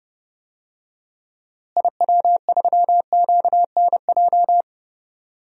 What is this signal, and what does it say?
Morse code, heard as a keyed tone, about 32 wpm, 700 Hz. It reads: IW3QNJ
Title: DXCC-100 Call Signs - 6 Characters - Repeated 3x - 30wpm